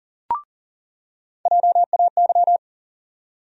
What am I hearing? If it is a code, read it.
Morse code, heard as a keyed tone, about 40 wpm, 700 Hz. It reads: JAY
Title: USA Call Sign Postfixes - 3 Letters 40wpm